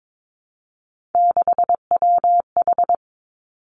Morse code, heard as a keyed tone, about 22 wpm, 700 Hz. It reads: 6WH
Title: Three Challenging Letter-Number Combinations 22wpm